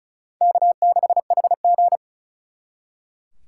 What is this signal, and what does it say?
Morse code, heard as a keyed tone, about 45 wpm, 700 Hz. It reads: K6HG